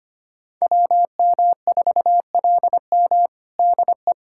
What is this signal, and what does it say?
Morse code, heard as a keyed tone, about 25 wpm, 700 Hz. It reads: WM4LM DE
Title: USA Callsign DE Callsign - 5 Characters - 25wpm